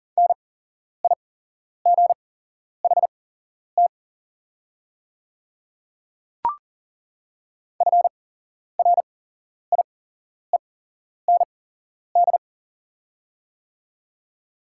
Morse code, heard as a keyed tone, about 40 wpm, spaced as for 11 wpm, 700 Hz. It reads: NIGHT FRIEND
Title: 8x Character Spacing - Top 300 Words - 40wpm